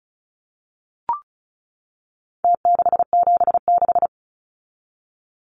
Morse code, heard as a keyed tone, about 34 wpm, 700 Hz. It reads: T676
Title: Numbers - Contest Style - 4-Digit 35wpm